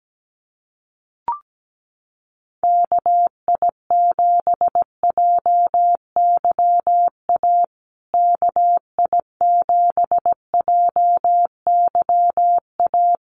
Morse code, heard as a keyed tone, about 17 wpm, 700 Hz. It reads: KI7JYA KI7JYA
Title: USA Call Signs Repeated 3x 17wpm